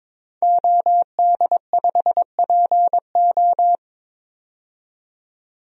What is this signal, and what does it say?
Morse code, heard as a keyed tone, about 23 wpm, 700 Hz. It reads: OD5PO
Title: DXCC-100 Call Signs - 5 Characters - 22wpm